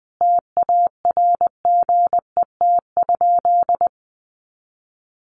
Morse code, heard as a keyed tone, about 20 wpm, 700 Hz. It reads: TARGET?
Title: Sentences from Top 1200 Words 20wpm